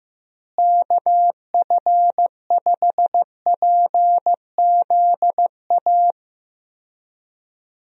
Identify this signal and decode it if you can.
Morse code, heard as a keyed tone, about 15 wpm, 700 Hz. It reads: KF5PZA
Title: USA Call Signs - 6 Characters - Repeated 3x - 15wpm